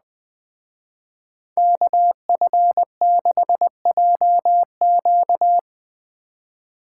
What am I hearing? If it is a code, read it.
Morse code, heard as a keyed tone, about 20 wpm, 700 Hz. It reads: KF6JQ
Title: Speed Racing - USA 2x2 Call Signs 17wpm